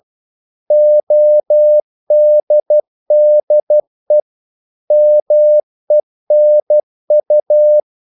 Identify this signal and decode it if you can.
Morse code, heard as a keyed tone, about 12 wpm, 600 Hz. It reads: ODDE MENU
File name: Korte ord DK 12wpm